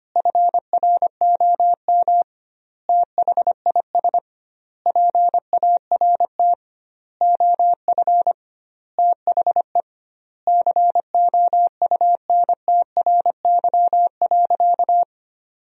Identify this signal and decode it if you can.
Morse code, heard as a keyed tone, about 25 wpm, 700 Hz. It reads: FROM THIS PART OF THE COUNTRY.
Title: 2x Word-Spacing - Sentences from Top 300 Words - 25wpm